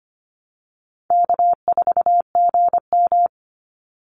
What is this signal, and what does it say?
Morse code, heard as a keyed tone, about 25 wpm, 700 Hz. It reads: K4GM
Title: Easy Call Signs Repeated 2x - 25wpm